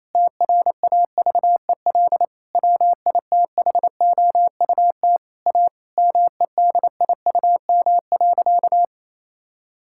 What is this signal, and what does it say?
Morse code, heard as a keyed tone, about 28 wpm, 700 Hz. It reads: TRAVEL WITHOUT A MEDIUM.
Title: Sentences from Top 1000 Words - Review Part 6 - 28wpm